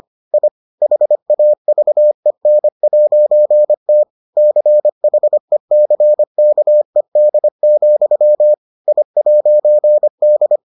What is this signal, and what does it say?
Morse code, heard as a keyed tone, about 25 wpm, 600 Hz. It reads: I HAVEN'T CHECKED, I'D